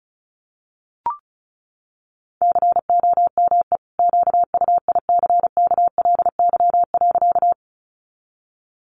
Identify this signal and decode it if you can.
Morse code, heard as a keyed tone, about 34 wpm, 700 Hz. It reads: COME QUICKLY.
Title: Sentences from Top 700 Words 35wpm